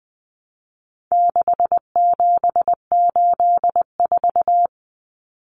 Morse code, heard as a keyed tone, about 20 wpm, 700 Hz. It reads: 6784
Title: Common Cut Numbers - 4-Digit 20wpm